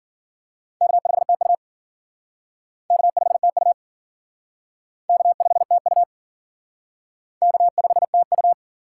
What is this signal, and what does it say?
Morse code, heard as a keyed tone, about 50 wpm, 700 Hz. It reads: K5TU K5TU K5TU K5TU